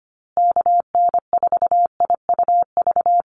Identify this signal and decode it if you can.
Morse code, heard as a keyed tone, about 25 wpm, 700 Hz. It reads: KN4IUV
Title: USA Call Signs Repeated 3x 25wpm